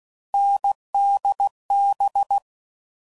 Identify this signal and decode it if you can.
Morse code, heard as a keyed tone, about 16 wpm, 800 Hz. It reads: NDB
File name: Combinations / Difficult Combos / awj1 ndb6 gz7